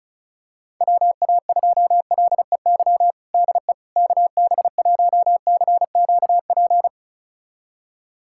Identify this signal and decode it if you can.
Morse code, heard as a keyed tone, about 34 wpm, 700 Hz. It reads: WA2LEY DE KB1CQP
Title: USA Callsign DE Callsign 35wpm